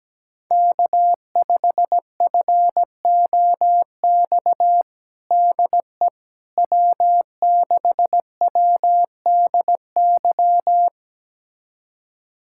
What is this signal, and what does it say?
Morse code, heard as a keyed tone, about 17 wpm, 700 Hz. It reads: K5FOX DE W6WDY
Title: USA Callsign DE Callsign - 5 Characters - 17wpm